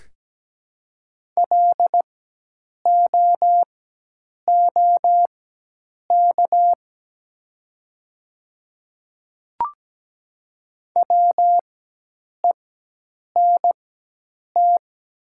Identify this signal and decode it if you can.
Morse code, heard as a keyed tone, about 17 wpm, spaced as for 8 wpm, 700 Hz. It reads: LOOK WENT
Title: 4x Character Spacing - Top 200 Words - 17wpm